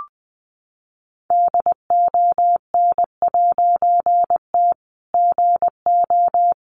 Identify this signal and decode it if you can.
Morse code, heard as a keyed tone, about 20 wpm, 700 Hz. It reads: DON'T GO
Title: A Sound Of Thunder 20wpm